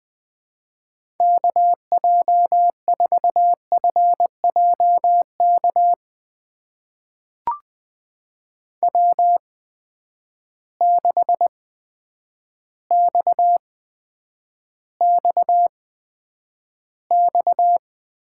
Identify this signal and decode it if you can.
Morse code, heard as a keyed tone, about 20 wpm, spaced as for 9 wpm, 700 Hz. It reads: KJ4FJK W6XXX
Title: USA Call Signs - 8x Character Spacing - 20wpm